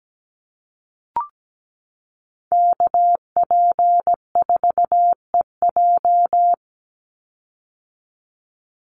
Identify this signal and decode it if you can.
Morse code, heard as a keyed tone, about 17 wpm, 700 Hz. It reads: KP4EJ
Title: Call Signs - 17wpm